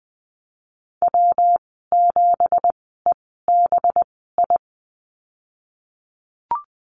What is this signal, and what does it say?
Morse code, heard as a keyed tone, about 20 wpm, spaced as for 15 wpm, 700 Hz. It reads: W7EBI
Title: 219 Lesson 35g - Number 0 - Call Signs 20wpm